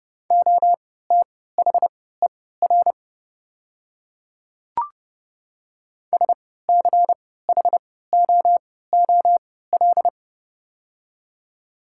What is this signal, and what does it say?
Morse code, heard as a keyed tone, about 30 wpm, spaced as for 17 wpm, 700 Hz. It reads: OTHER SCHOOL
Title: 3x Character Spacing - Top 300 Words - 30wpm